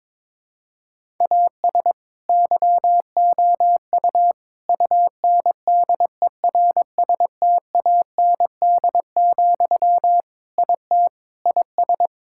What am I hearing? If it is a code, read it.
Morse code, heard as a keyed tone, about 23 wpm, 700 Hz. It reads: AS YOU UNDERSTAND, IT IS
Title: Sentences from Top 1000 Words - Review Part 5 - 22wpm